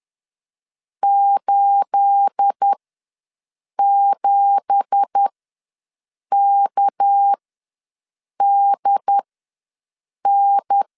Morse code, heard as a keyed tone, about 10 wpm, spaced as for 6 wpm, 800 Hz. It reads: 87KDN
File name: LEZIONI / 10 SIGNS / 10A SLASH ws 10wpm 4121